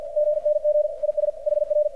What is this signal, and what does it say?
Morse code, heard as a keyed tone, about 53 wpm, 600 Hz, with radio noise. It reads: GNG IS HR
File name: archived / CRSnet